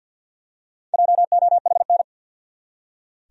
Morse code, heard as a keyed tone, about 47 wpm, 700 Hz. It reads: JOHN
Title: USA Common Names - 50wpm